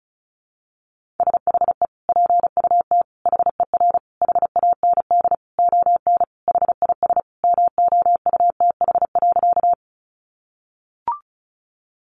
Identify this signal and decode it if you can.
Morse code, heard as a keyed tone, about 34 wpm, 700 Hz. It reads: SHE PUT HER HAND ON HIS MOUTH.
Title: Sentences from Top 1500 Words 35wpm